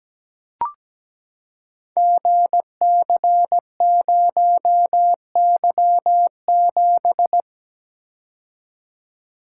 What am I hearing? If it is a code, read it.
Morse code, heard as a keyed tone, about 17 wpm, 700 Hz. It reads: GC0Y7
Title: Code Groups - 5 Letter:Numbers - 17wpm